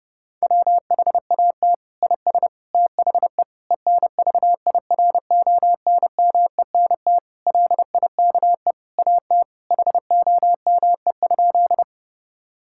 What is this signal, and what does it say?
Morse code, heard as a keyed tone, about 30 wpm, 700 Hz. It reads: WHAT IS THE ENVIRONMENT LIKE AT HOME?
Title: Sentences from Top 1000 Words - Review Part 5 - 30wpm